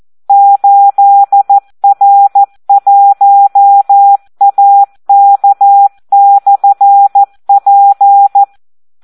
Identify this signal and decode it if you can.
Morse code, heard as a keyed tone, about 14 wpm, 800 Hz. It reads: 8R1AK/P
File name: Audio practica CW / SD 1 14